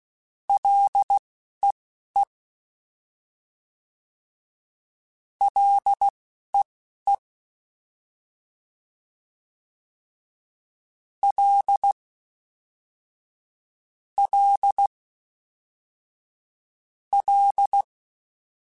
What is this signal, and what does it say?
Morse code, heard as a keyed tone, about 16 wpm, spaced as for 11 wpm, 800 Hz. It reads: LEE LEE L L L
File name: Ls / Single L m3s / Single L Prefixe 3x2 Twice @10